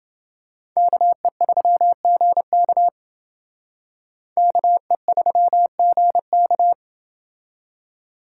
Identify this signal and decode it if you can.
Morse code, heard as a keyed tone, about 28 wpm, 700 Hz. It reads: KE3GK KE3GK